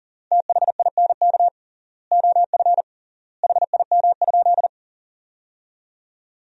Morse code, heard as a keyed tone, about 40 wpm, 700 Hz. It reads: THINK OF HIM?
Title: 3x Word-Spacing - Sentences from Top 200 Words - 40wpm